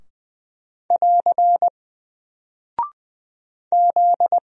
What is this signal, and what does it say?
Morse code, heard as a keyed tone, about 20 wpm, 700 Hz. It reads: <AR> Z
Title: Single Character (Letter:Number:Punctuation:Prosign) 20wpm